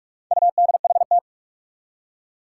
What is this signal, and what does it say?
Morse code, heard as a keyed tone, about 44 wpm, 700 Hz. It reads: UDHT
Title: Four Challenging Letter-Number Combinations 45wpm